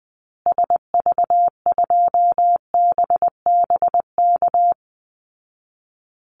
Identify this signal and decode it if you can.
Morse code, heard as a keyed tone, about 19 wpm, 700 Hz. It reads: SV2BBK
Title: Call Signs Repeated 2x - 20wpm